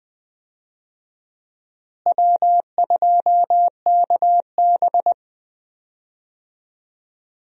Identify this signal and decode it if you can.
Morse code, heard as a keyed tone, about 20 wpm, 700 Hz. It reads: W2KB
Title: USA Call Signs - 4 Characters - Repeated 2x - 20wpm